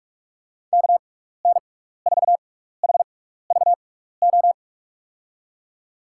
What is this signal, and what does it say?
Morse code, heard as a keyed tone, about 46 wpm, spaced as for 15 wpm, 700 Hz. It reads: KN4HVO
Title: USA Call Signs - 6 Characters - 6x Character Spacing - 45wpm